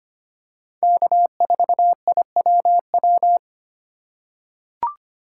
Morse code, heard as a keyed tone, about 25 wpm, 700 Hz. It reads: K4IWW
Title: USA Call Signs - 5 Characters - 2x Character Spacing - 25wpm